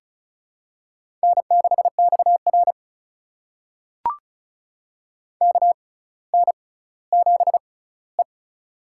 Morse code, heard as a keyed tone, about 36 wpm, spaced as for 18 wpm, 700 Hz. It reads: N6XR KN7E
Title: USA Call Signs - 4 Characters - 6x Character Spacing - 35wpm